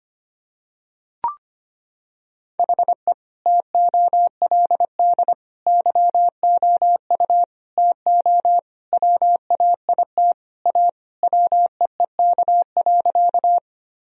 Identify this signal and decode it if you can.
Morse code, heard as a keyed tone, about 25 wpm, 700 Hz. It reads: HE TOLD YOU TO WAIT A WEEK.
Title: Sentences from Top 1000 Words - Review Part 2 - 25wpm